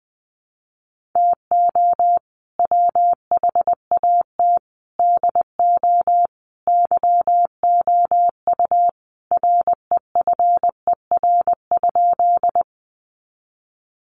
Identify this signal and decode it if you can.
Morse code, heard as a keyed tone, about 20 wpm, 700 Hz. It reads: TO WHAT DO YOU REFER?
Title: Sentences from Top 1000 Words 20wpm